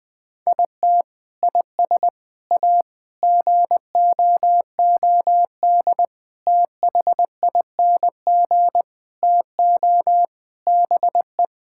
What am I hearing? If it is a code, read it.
Morse code, heard as a keyed tone, about 20 wpm, 700 Hz. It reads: IT IS A GOOD THING TO BE
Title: Sentences from Top 1000 Words - Review Part 5 - 20wpm